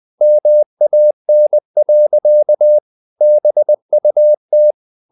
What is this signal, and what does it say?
Morse code, heard as a keyed tone, about 20 wpm, 600 Hz. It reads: MAN. BUT
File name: the old man and the sea part 2 20wpm